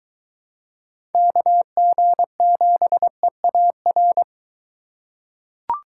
Morse code, heard as a keyed tone, about 21 wpm, 700 Hz. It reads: KG7EAR